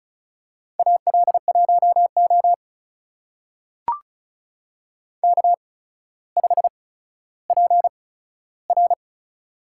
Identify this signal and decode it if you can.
Morse code, heard as a keyed tone, about 36 wpm, spaced as for 15 wpm, 700 Hz. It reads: AL1O K5PR
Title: USA Call Signs - 4 Characters - 8x Character Spacing - 35wpm